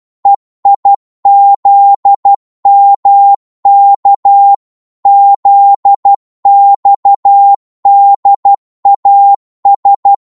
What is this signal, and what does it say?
Morse code, heard as a keyed tone, about 12 wpm, 800 Hz. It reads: EIZMK ZXDAS
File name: Morse / ltr-12wpm